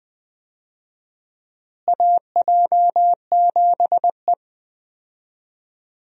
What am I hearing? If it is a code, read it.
Morse code, heard as a keyed tone, about 20 wpm, 700 Hz. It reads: AJ7E
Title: USA Call Signs - 4 Characters - Repeated 2x - 20wpm